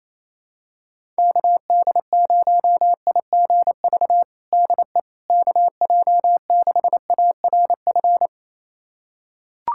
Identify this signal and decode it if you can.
Morse code, heard as a keyed tone, about 27 wpm, 700 Hz. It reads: KD0IGV DE KJ6ARF
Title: USA Callsign DE Callsign - 6 Characters - 28wpm